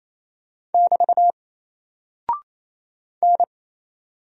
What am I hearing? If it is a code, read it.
Morse code, heard as a keyed tone, about 27 wpm, 700 Hz. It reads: <BT> N
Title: Single Character (Letter:Number:Punctuation:Prosign) 28wpm